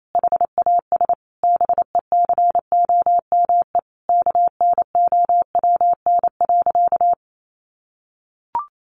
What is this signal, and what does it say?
Morse code, heard as a keyed tone, about 28 wpm, 700 Hz. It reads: HAS BECOME KNOWN.
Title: Sentences from Top 1300 Words 28wpm